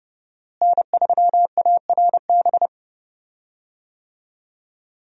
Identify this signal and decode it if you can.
Morse code, heard as a keyed tone, about 31 wpm, 700 Hz. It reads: N3ARB
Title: USA Call Signs - 5 Characters - Repeated 2x - 30wpm